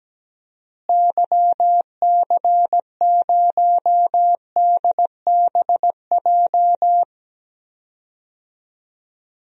Morse code, heard as a keyed tone, about 17 wpm, 700 Hz. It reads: YC0DBJ
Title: DXCC-100 Call Signs - 6 Characters - Repeated 2x - 17wpm